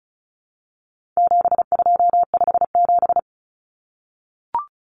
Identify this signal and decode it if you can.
Morse code, heard as a keyed tone, about 34 wpm, 700 Hz. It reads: 7257
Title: Numbers - 4-Digit 35wpm